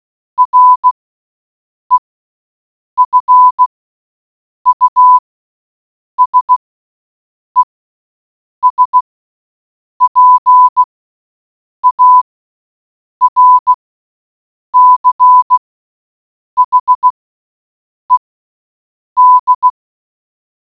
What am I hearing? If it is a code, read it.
Morse code, heard as a keyed tone, about 16 wpm, spaced as for 7 wpm, 1000 Hz. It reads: REFUSES PARCHED